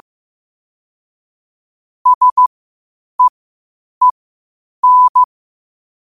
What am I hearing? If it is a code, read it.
Morse code, heard as a keyed tone, about 15 wpm, spaced as for 8 wpm, 1000 Hz. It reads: SEEN
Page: qotd-at-8-wpm